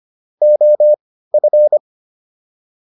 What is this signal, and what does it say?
Morse code, heard as a keyed tone, about 25 wpm, spaced as for 15 wpm, 600 Hz. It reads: OF